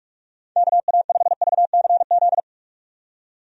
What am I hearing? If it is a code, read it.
Morse code, heard as a keyed tone, about 44 wpm, 700 Hz. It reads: KA5VCZ
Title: USA Call Signs - 6 Characters - 45wpm